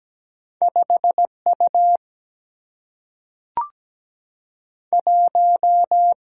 Morse code, heard as a keyed tone, about 17 wpm, 700 Hz. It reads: 5 U 1
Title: Single-Letter Single-Number 17wpm